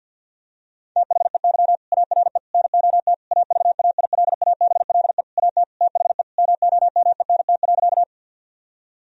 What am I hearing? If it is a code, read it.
Morse code, heard as a keyed tone, about 50 wpm, 700 Hz. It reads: THEY ARE NOT AVAILABLE AT THE MOMENT.
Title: Sentences from Top 1000 Words - Review Part 3 - 50wpm